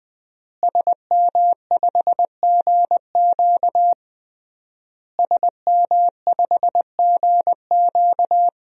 Morse code, heard as a keyed tone, about 20 wpm, 700 Hz. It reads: SM5GQ SM5GQ
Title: DXCC-100 Call Signs - 5 Characters - Repeated 3x - 20wpm